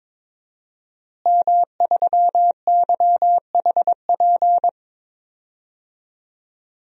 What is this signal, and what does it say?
Morse code, heard as a keyed tone, about 22 wpm, 700 Hz. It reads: M3YHP
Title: DXCC-100 Call Signs - 5 Characters - Repeated 2x - 22wpm